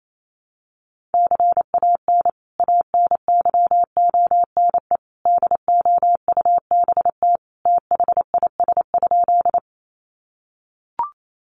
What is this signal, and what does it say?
Morse code, heard as a keyed tone, about 28 wpm, 700 Hz. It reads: CAN ANYONE DOUBT THIS?
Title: Sentences from Top 1500 Words 28wpm